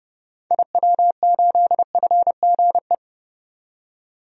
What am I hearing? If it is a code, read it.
Morse code, heard as a keyed tone, about 31 wpm, 700 Hz. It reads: IW8FGE
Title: DXCC-100 Call Signs - 6 Characters - 30wpm